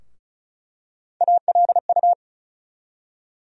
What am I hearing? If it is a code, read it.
Morse code, heard as a keyed tone, about 36 wpm, 700 Hz. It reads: ALU
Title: USA Call Sign Postfixes - 3 Letters 35wpm